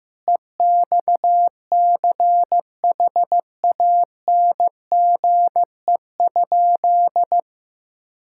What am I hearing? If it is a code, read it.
Morse code, heard as a keyed tone, about 15 wpm, 700 Hz. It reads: EXCHANGE?
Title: Sentences from Top 1000 Words - Review Part 6 - 15wpm